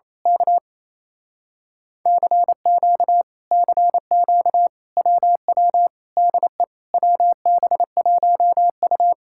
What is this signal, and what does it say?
Morse code, heard as a keyed tone, about 27 wpm, 700 Hz. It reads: K CQ CQ WW DE WB1U
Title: Speed-Racing Common QSO Phrases - 28wpm